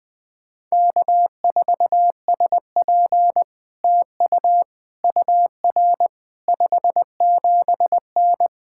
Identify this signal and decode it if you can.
Morse code, heard as a keyed tone, about 19 wpm, 700 Hz. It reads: K4SP TU UR 57N
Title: POTA QSO Practice - 20wpm